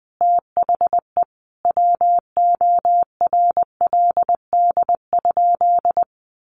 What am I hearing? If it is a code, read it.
Morse code, heard as a keyed tone, about 20 wpm, 700 Hz. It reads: THE WORLD?
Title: Sentences from Top 200 Words 20wpm